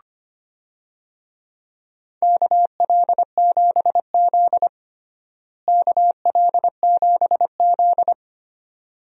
Morse code, heard as a keyed tone, about 25 wpm, 700 Hz. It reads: KL7Z KL7Z
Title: DXCC-100 Call Signs - 4 Characters - Repeated 2x - 25wpm